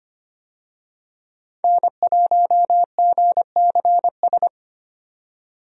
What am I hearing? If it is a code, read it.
Morse code, heard as a keyed tone, about 25 wpm, 700 Hz. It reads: N1GCS
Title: USA Call Signs - 5 Characters - 25wpm